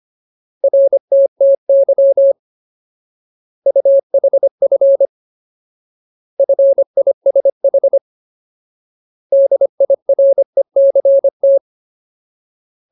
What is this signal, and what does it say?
Morse code, heard as a keyed tone, about 25 wpm, 550 Hz. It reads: RTTY UHF FISH DIRECT
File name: Ultimate Word List / 25WPM Ultimate